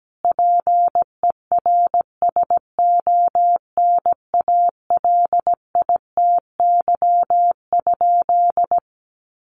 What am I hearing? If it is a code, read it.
Morse code, heard as a keyed tone, about 17 wpm, 700 Hz. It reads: PERSONALITY?
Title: Sentences from Top 1400 Words 17wpm